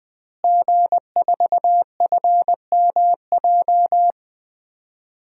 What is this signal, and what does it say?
Morse code, heard as a keyed tone, about 20 wpm, 700 Hz. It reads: G4FMJ
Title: DXCC-100 Call Signs - 5 Characters - Repeated 3x - 20wpm